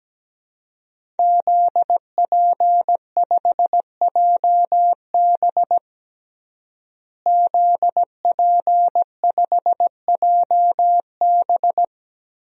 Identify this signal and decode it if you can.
Morse code, heard as a keyed tone, about 17 wpm, 700 Hz. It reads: ZP5JB ZP5JB
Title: DXCC-100 Call Signs - 5 Characters - Repeated 2x - 17wpm